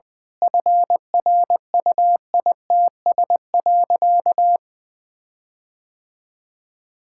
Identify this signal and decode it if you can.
Morse code, heard as a keyed tone, about 20 wpm, 700 Hz. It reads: FRUITS.
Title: Sentences with 7-Word Phrases - Spoken Beginning - 20wpm